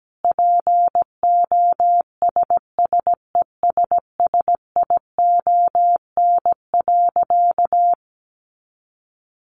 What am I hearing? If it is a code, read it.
Morse code, heard as a keyed tone, about 17 wpm, 700 Hz. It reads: POSSESSION.
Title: Sentences from Top 1400 Words 17wpm